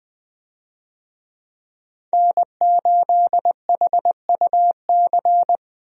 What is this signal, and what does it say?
Morse code, heard as a keyed tone, about 21 wpm, 700 Hz. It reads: N8HUC
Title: USA Call Signs - 5 Characters - Repeated 3x - 20wpm